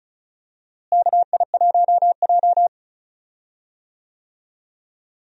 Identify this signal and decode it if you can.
Morse code, heard as a keyed tone, about 36 wpm, 700 Hz. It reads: KI1J
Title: USA Call Signs - 4 Characters - Repeated 2x - 35wpm